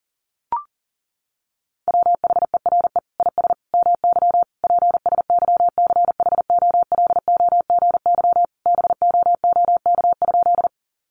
Morse code, heard as a keyed tone, about 40 wpm, 700 Hz. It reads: WHERE IS MY PSYCHOLOGY BOOK?
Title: Sentences from Top 1400 Words 40wpm